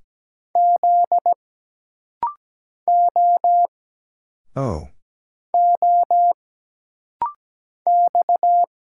Morse code, heard as a keyed tone, about 17 wpm, 700 Hz. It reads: Z O O X
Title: Single Letter-Number - Rapid-Fire 17wpm